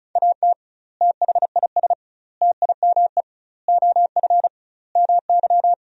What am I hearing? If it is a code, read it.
Morse code, heard as a keyed tone, about 34 wpm, 700 Hz. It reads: AT THIS TIME OF MY
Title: 2x Word-Spacing - Sentences from Top 300 Words - 35wpm